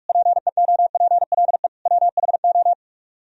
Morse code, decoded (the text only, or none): PEOPLE WHO